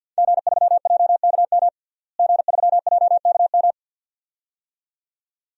Morse code, heard as a keyed tone, about 47 wpm, 700 Hz. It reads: G3JKM G3JKM
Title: DXCC-100 Call Signs - 5 Characters - Repeated 2x - 50wpm